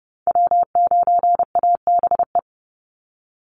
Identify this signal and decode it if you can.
Morse code, heard as a keyed tone, about 30 wpm, 700 Hz. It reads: W9ABE
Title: USA Call Signs 30wpm